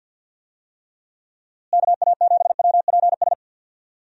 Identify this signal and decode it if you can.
Morse code, heard as a keyed tone, about 46 wpm, 700 Hz. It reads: KA7WPS
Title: USA Call Signs - 6 Characters - Repeated 2x - 50wpm